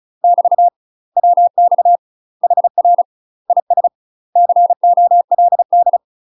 Morse code, heard as a keyed tone, about 36 wpm, 700 Hz. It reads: <BT> WX HR IS COLD
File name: UK MASTER QSOs TRAINING / 35WPM / UK MASTER QSOs 0041